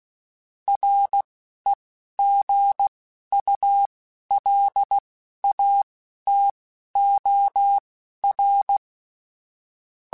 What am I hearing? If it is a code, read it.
Morse code, heard as a keyed tone, about 16 wpm, spaced as for 11 wpm, 800 Hz. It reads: REGULATOR
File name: Technical Subjects / Technical Subjects-mp3s / Sw Pwr Sup @10